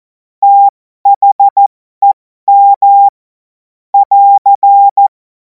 Morse code, heard as a keyed tone, about 14 wpm, spaced as for 12 wpm, 800 Hz. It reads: THEM <AR>
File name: Morsekurs / Witze